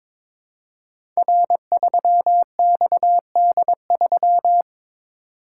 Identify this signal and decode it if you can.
Morse code, heard as a keyed tone, about 22 wpm, 700 Hz. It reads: R3XD3
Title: Code Groups - 5 Letter:Numbers - 22wpm